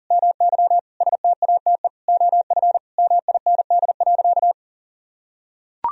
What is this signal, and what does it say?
Morse code, heard as a keyed tone, about 37 wpm, 700 Hz. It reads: MY STATE OF MIND.
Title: Sentences from Top 1000 Words - Review Part 2 - 40wpm